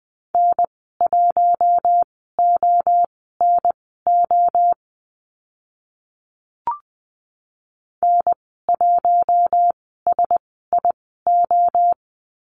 Morse code, heard as a keyed tone, about 20 wpm, spaced as for 14 wpm, 700 Hz. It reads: N1ONO N1SIO
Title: 055 Lesson 10g - Letter r - Call Signs 20wpm